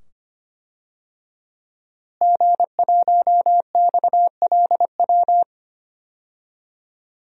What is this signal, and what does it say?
Morse code, heard as a keyed tone, about 25 wpm, 700 Hz. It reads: G1XLW
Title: DXCC-100 Call Signs - 5 Characters - Repeated 2x - 25wpm